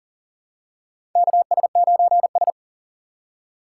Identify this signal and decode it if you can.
Morse code, heard as a keyed tone, about 40 wpm, 700 Hz. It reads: KS9S
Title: USA Call Signs - 4 Characters - 6x Character Spacing - 40wpm